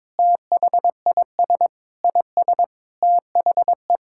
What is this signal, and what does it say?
Morse code, heard as a keyed tone, about 22 wpm, 700 Hz. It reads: THIS IS THE
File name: Sets of 9 Words - 22wpm